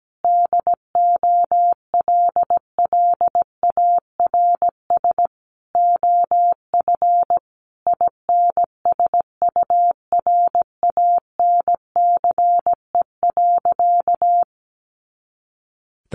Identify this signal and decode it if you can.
Morse code, heard as a keyed tone, about 17 wpm, 700 Hz. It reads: DOLLARS OF INSURANCE.
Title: A Sound Of Thunder 17wpm